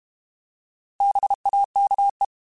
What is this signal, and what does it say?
Morse code, heard as a keyed tone, about 32 wpm, 800 Hz. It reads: BAKE
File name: Alphabetical / Alphabetical mp3s / Bs @30